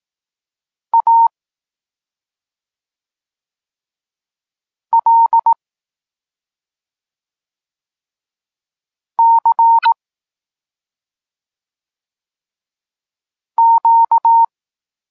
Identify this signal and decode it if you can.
Morse code, heard as a keyed tone, about 18 wpm, 920 Hz. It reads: A L C Q